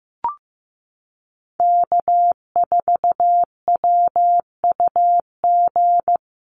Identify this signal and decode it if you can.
Morse code, heard as a keyed tone, about 15 wpm, 700 Hz. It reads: K4WUG
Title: Common QSO Phrases Encore 15wpm